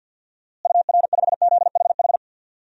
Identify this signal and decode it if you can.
Morse code, heard as a keyed tone, about 50 wpm, 700 Hz. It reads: UR5ZHH
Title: DXCC-100 Call Signs - 6 Characters - Repeated 3x - 50wpm